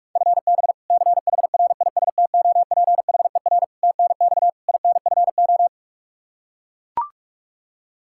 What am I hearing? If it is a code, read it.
Morse code, heard as a keyed tone, about 41 wpm, 700 Hz. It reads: FB CHRISTOPHER TNX INFO
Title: K1USN SST QSO Practice 45wpm